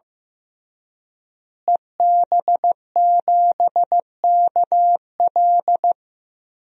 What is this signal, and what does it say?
Morse code, heard as a keyed tone, about 15 wpm, 700 Hz. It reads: EB7KL
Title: DXCC-100 Call Signs - 5 Characters - Repeated 2x - 15wpm